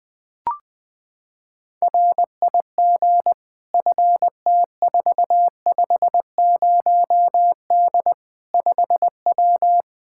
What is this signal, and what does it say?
Morse code, heard as a keyed tone, about 21 wpm, 700 Hz. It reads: RIG FT450D 5W
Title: Practice QSO - Exchange 2 - Info - 20wpm